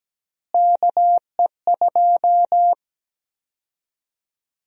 Morse code, heard as a keyed tone, about 17 wpm, 700 Hz. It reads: KE2
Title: USA Call Sign Prefixes - Repeated 2x - 17wpm